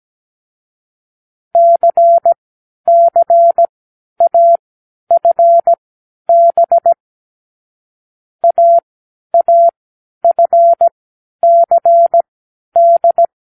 Morse code, heard as a keyed tone, about 17 wpm, spaced as for 10 wpm, 680 Hz. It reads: CCAFB AAFCD
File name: k7qo / Lesson9(A-F)